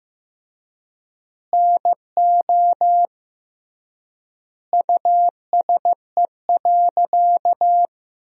Morse code, heard as a keyed tone, about 16 wpm, 700 Hz. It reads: NO USE.
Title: 3x Word-Spacing - Sentences from Top 100 Words - 15wpm